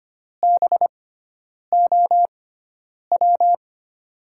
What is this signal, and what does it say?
Morse code, heard as a keyed tone, about 25 wpm, spaced as for 9 wpm, 700 Hz. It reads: BOW
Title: USA Call Signs - 6 Characters - 6x Character Spacing - 25wpm